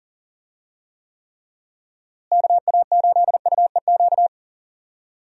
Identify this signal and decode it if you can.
Morse code, heard as a keyed tone, about 40 wpm, 700 Hz. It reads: KA8UEQ
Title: USA Call Signs - 6 Characters - Repeated 3x - 40wpm